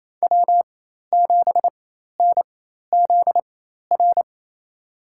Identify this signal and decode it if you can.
Morse code, heard as a keyed tone, about 28 wpm, spaced as for 13 wpm, 700 Hz. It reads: W7NZR
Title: USA Call Signs - 5 Characters - 4x Character Spacing - 28wpm